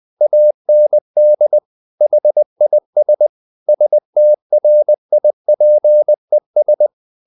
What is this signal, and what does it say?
Morse code, heard as a keyed tone, about 20 wpm, 600 Hz. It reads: AND HIS STRIPES
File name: the old man and the sea part 2 20wpm